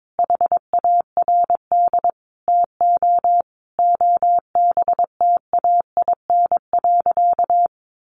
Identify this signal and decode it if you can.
Morse code, heard as a keyed tone, about 22 wpm, 700 Hz. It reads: HARD TO OBTAIN.
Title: Sentences from Top 1300 Words 22wpm